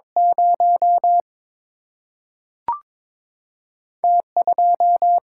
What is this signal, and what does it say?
Morse code, heard as a keyed tone, about 22 wpm, 700 Hz. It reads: 0 T 2
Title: Single-Letter Single-Number 22wpm